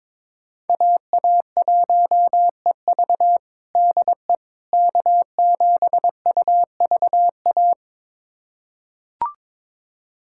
Morse code, heard as a keyed tone, about 22 wpm, 700 Hz. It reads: AA1EV DE K7UVA
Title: USA Callsign DE Callsign - 5 Characters - 22wpm